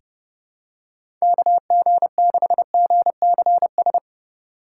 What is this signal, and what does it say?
Morse code, heard as a keyed tone, about 30 wpm, 700 Hz. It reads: KG6GCS
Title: USA Call Signs - 6 Characters - 3x Character Spacing - 30wpm